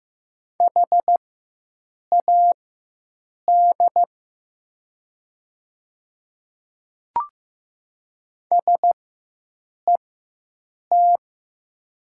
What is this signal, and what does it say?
Morse code, heard as a keyed tone, about 15 wpm, spaced as for 7 wpm, 700 Hz. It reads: HAD SET
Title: 4x Character Spacing - Top 200 Words - 15wpm